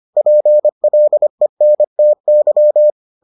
Morse code, heard as a keyed tone, about 25 wpm, 600 Hz. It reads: PLENTY